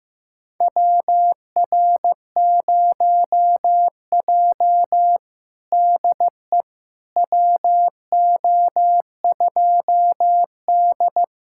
Morse code, heard as a keyed tone, about 15 wpm, 700 Hz. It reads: WR0J DE WO2D
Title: USA Callsign DE Callsign - 4 Characters - 15wpm